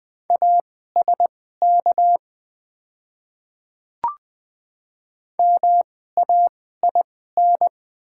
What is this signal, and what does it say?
Morse code, heard as a keyed tone, about 20 wpm, spaced as for 14 wpm, 690 Hz. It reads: ASK MAIN
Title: 2x Character Spacing - Top 500 Words - 20wpm